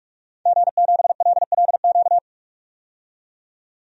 Morse code, heard as a keyed tone, about 46 wpm, 700 Hz. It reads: G7LLQ
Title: DXCC-100 Call Signs - 5 Characters - 45wpm